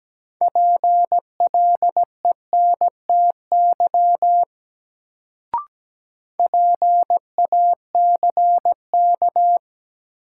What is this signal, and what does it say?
Morse code, heard as a keyed tone, about 17 wpm, 700 Hz. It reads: PLENTY PACK
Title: Sentences from Top 1300 Words - New Word Drill Alt - 17wpm